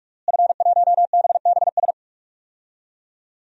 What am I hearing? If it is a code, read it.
Morse code, heard as a keyed tone, about 46 wpm, 700 Hz. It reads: F1BBS
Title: DXCC-100 Call Signs - 5 Characters - Repeated 3x - 45wpm